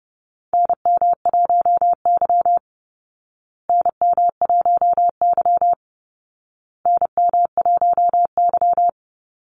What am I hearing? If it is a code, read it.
Morse code, heard as a keyed tone, about 30 wpm, 700 Hz. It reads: NM1Y NM1Y NM1Y
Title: Call Signs Repeated 2x - 30wpm